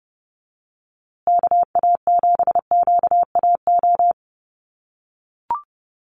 Morse code, heard as a keyed tone, about 30 wpm, 700 Hz. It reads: KA7QAO
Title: USA Call Signs Repeated 2x 30wpm